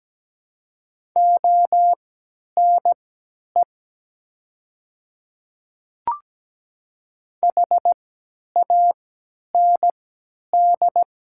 Morse code, heard as a keyed tone, about 17 wpm, spaced as for 10 wpm, 700 Hz. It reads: ONE HAND
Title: 3x Character Spacing - Top 500 Words - 17wpm